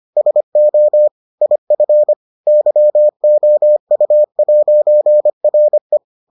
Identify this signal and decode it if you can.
Morse code, heard as a keyed tone, about 25 wpm, 600 Hz. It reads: SO IF YOU'RE